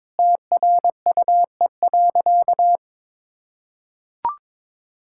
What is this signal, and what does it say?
Morse code, heard as a keyed tone, about 22 wpm, 700 Hz. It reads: TRUE.
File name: practice / 22wpm / Sentences with 4-Word Phrases - 22wpm